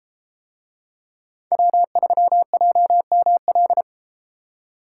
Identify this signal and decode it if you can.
Morse code, heard as a keyed tone, about 32 wpm, 700 Hz. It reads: W3JML